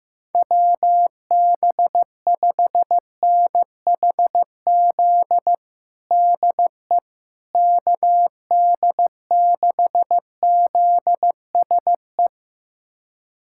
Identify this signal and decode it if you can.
Morse code, heard as a keyed tone, about 15 wpm, 700 Hz. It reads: WB5NHZ DE KD6ZSE
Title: USA Callsign DE Callsign 15wpm